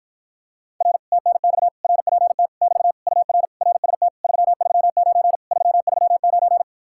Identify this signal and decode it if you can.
Morse code, heard as a keyed tone, about 54 wpm, 700 Hz. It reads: R TNX RPT <BT> UR RST 339 339